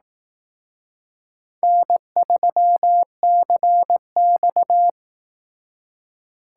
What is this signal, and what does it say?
Morse code, heard as a keyed tone, about 19 wpm, 700 Hz. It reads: N3CX